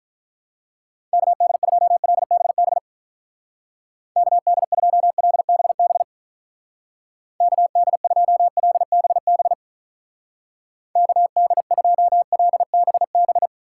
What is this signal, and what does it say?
Morse code, heard as a keyed tone, about 45 wpm, 700 Hz. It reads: KD2LBB KD2LBB KD2LBB KD2LBB